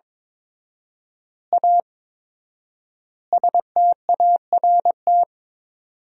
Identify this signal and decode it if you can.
Morse code, heard as a keyed tone, about 22 wpm, 700 Hz. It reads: A START
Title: 4x Word-Spacing - Sentences from Top 300 Words - 22wpm